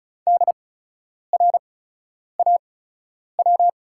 Morse code, heard as a keyed tone, about 34 wpm, spaced as for 10 wpm, 700 Hz. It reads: DRAW
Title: 8x Character Spacing - Top 300 Words - 35wpm